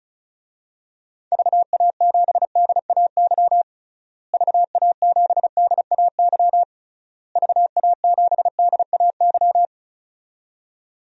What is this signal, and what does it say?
Morse code, heard as a keyed tone, about 36 wpm, 700 Hz. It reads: VA7DAY VA7DAY VA7DAY
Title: DXCC-100 Call Signs - 6 Characters - Repeated 3x - 35wpm